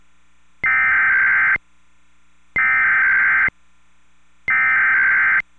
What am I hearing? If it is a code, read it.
Emergency Alert System (SAME) tones (header bursts), heard as harsh buzzing bursts. It is heard from 0.7 to 5.4 s.